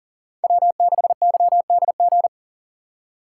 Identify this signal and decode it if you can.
Morse code, heard as a keyed tone, about 50 wpm, 700 Hz. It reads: W6YDG